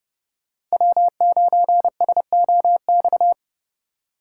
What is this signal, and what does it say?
Morse code, heard as a keyed tone, about 37 wpm, 700 Hz. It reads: W9SOX
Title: Speed Racing - USA 1x3 Call Signs 30wpm